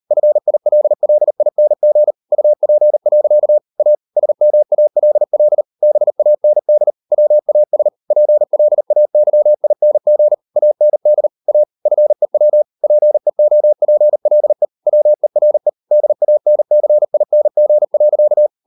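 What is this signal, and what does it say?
Morse code, heard as a keyed tone, about 38 wpm, 600 Hz. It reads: FILLING UP. A SMALL BAND WAS PLAYING AND A FEW PEOPLE WERE DANCING.
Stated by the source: Morse code practice at 38 WPM, sending Ken Follett's novel Never in English, from CW med Gnister.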